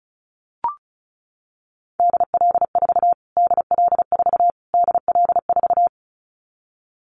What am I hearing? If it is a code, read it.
Morse code, heard as a keyed tone, about 34 wpm, 700 Hz. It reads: DL4 DL4 DL4
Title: Callsign Prefixes for Top 25 DXCC - Repeated 3x - 35wpm